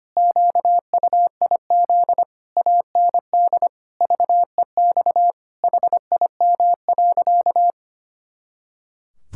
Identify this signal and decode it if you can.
Morse code, heard as a keyed tone, about 25 wpm, 700 Hz. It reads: QUIZ AND VEX HIM.
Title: Pangram Challenge v2 - 25wpm